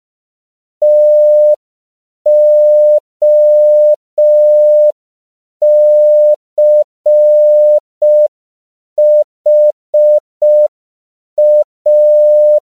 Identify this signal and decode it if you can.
Morse code, heard as a keyed tone, about 5 wpm, 600 Hz. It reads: TOCHA
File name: CW / 5PPM